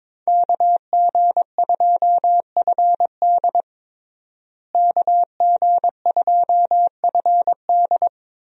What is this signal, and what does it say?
Morse code, heard as a keyed tone, about 21 wpm, 700 Hz. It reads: KG2FD KG2FD
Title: USA Call Signs - 5 Characters - Repeated 2x - 22wpm